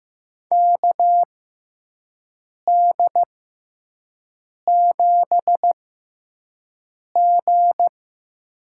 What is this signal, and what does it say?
Morse code, heard as a keyed tone, about 15 wpm, spaced as for 5 wpm, 700 Hz. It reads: KD7G
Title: USA Call Signs - 6x Character Spacing - 15wpm